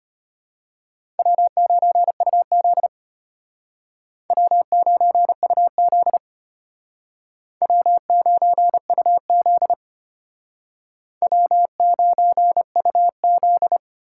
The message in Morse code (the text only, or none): W9UZ W9UZ W9UZ W9UZ